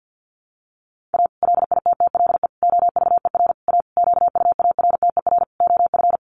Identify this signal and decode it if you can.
Morse code, heard as a keyed tone, about 50 wpm, 700 Hz. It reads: A LITTLE OVER A QUARTER OF
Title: Sentences from Top 1300 Words 50wpm